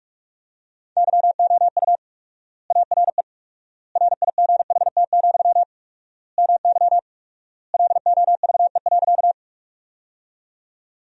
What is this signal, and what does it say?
Morse code, heard as a keyed tone, about 46 wpm, 700 Hz. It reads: YOU ARE RIGHT, MY LOVE.
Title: 4x Word-Spacing - Sentences from Top 400 Words - 45wpm